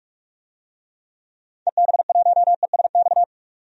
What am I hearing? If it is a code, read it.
Morse code, heard as a keyed tone, about 45 wpm, 700 Hz. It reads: EB1ESX
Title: DXCC-100 Call Signs - 6 Characters - 45wpm